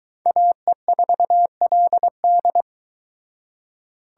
Morse code, heard as a keyed tone, about 22 wpm, 700 Hz. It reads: AE4LD